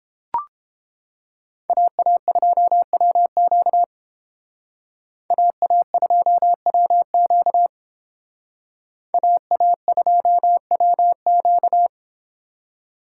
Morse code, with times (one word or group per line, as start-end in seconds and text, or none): 1.7-3.8 AA2WQ
5.3-7.7 AA2WQ
9.1-11.9 AA2WQ